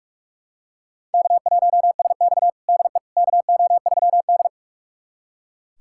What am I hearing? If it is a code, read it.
Morse code, heard as a keyed tone, about 42 wpm, 700 Hz. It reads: K1SX DE KO3D
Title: USA Callsign DE Callsign - 4 Characters - 45wpm